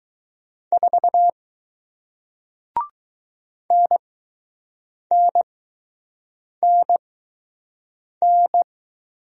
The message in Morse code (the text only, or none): 4 NNNN